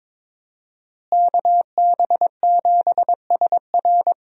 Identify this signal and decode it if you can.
Morse code, heard as a keyed tone, about 21 wpm, 700 Hz. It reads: KB7SR
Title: USA Call Signs - 5 Characters - Repeated 3x - 22wpm